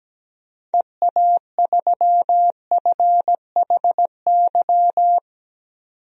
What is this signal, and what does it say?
Morse code, heard as a keyed tone, about 17 wpm, 700 Hz. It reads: EA3FHY
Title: DXCC-100 Call Signs - 6 Characters - 17wpm